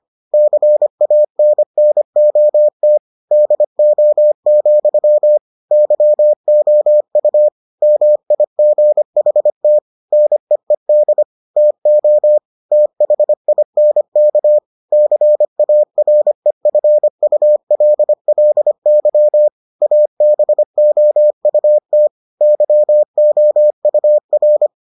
Morse code, heard as a keyed tone, about 25 wpm, 600 Hz. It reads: CANNOT DO, YOU MIGHT NEED TO THINK CAREFULLY ABOUT YOUR